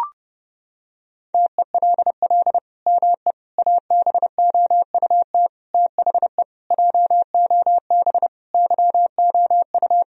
Morse code, heard as a keyed tone, about 30 wpm, 700 Hz. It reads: TELL ME ABOUT THE JOB YOU
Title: Sentences from Top 1000 Words - Review Part 3 - 30wpm